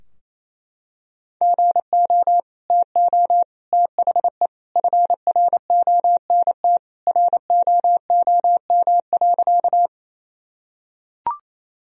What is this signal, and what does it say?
Morse code, heard as a keyed tone, about 28 wpm, 700 Hz. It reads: GO TO THE FRONT ROOM.
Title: Sentences from Top 1000 Words - Review Part 2 - 28wpm